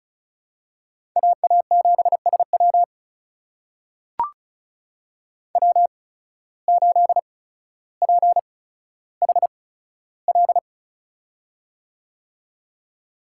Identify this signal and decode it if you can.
Morse code, heard as a keyed tone, about 36 wpm, spaced as for 15 wpm, 700 Hz. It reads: AA7SW W8PHL
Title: USA Call Signs - 5 Characters - 8x Character Spacing - 35wpm